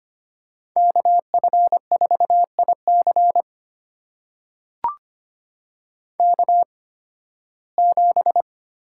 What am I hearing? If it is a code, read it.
Morse code, heard as a keyed tone, about 25 wpm, spaced as for 16 wpm, 700 Hz. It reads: KF4IC K7
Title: USA Call Signs - 5 Characters - 8x Character Spacing - 25wpm